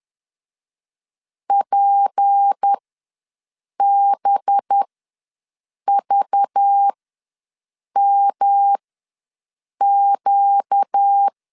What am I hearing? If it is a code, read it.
Morse code, heard as a keyed tone, about 10 wpm, spaced as for 6 wpm, 800 Hz. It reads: PBVMQ
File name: LEZIONI / 6 PKX0 / 6 P ws 10wpm 3751